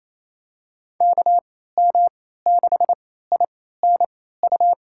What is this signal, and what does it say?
Morse code, heard as a keyed tone, about 28 wpm, spaced as for 16 wpm, 700 Hz. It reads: KM6INU
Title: USA Call Signs - 6 Characters - 3x Character Spacing - 28wpm